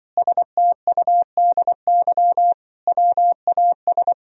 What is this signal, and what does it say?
Morse code, heard as a keyed tone, about 24 wpm, 700 Hz. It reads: STUDY WAS
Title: Speed-Racing - Sets of 5 Words 20wpm